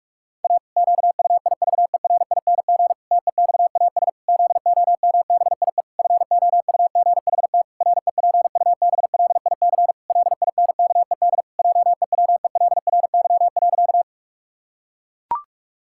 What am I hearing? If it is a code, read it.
Morse code, heard as a keyed tone, about 41 wpm, 700 Hz. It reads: A QUIVERING TEXAS ZOMBIE FOUGHT REPUBLIC LINKED JEWELRY.
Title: Pangram Challenge v2 - 45wpm